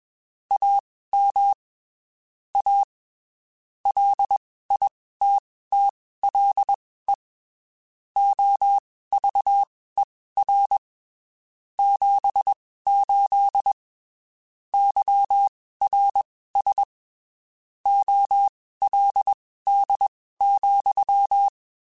Morse code, heard as a keyed tone, about 21 wpm, spaced as for 15 wpm, 800 Hz. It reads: AM A LITTLE OVER 78 YRS OLD,
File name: (spaced) / QSOs Spaced-mp3s / WA7QOX 2 @20 sp